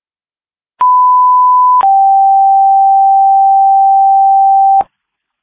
Two-tone page, heard as a one-second tone, then a three-second tone, about 1000 then 780 Hz. It is heard from 0.8 to 4.8 s.